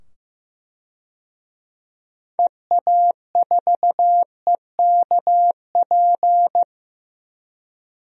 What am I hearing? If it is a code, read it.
Morse code, heard as a keyed tone, about 16 wpm, 700 Hz. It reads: EA4EKP
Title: DXCC-100 Call Signs - 6 Characters - 15wpm